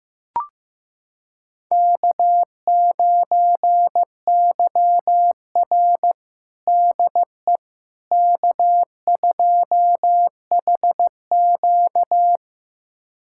Morse code, heard as a keyed tone, about 15 wpm, 700 Hz. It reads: K9YR DE K2HQ
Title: USA Callsign DE Callsign - 4 Characters - 15wpm